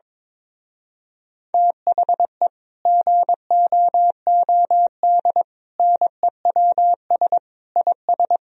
Morse code, heard as a keyed tone, about 22 wpm, 700 Hz. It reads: THE GOOD NEWS IS
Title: Speed-Racing - Sets of 5 Words 22wpm